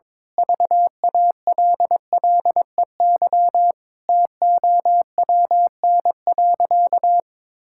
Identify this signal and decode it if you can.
Morse code, heard as a keyed tone, about 21 wpm, 700 Hz. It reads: VALLEY TOWN.
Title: Pangram Challenge v2 - 22wpm